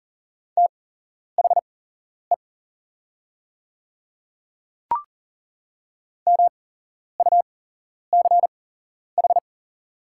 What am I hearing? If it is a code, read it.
Morse code, heard as a keyed tone, about 40 wpm, spaced as for 11 wpm, 690 Hz. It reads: THE MUCH
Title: 8x Character Spacing - Top 200 Words - 40wpm